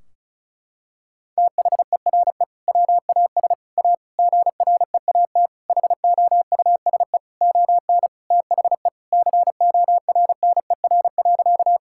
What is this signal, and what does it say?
Morse code, heard as a keyed tone, about 34 wpm, 700 Hz. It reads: THERE WAS A GREAT HOUSE ON THE CORNER.
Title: Sentences from Top 1600 Words 35wpm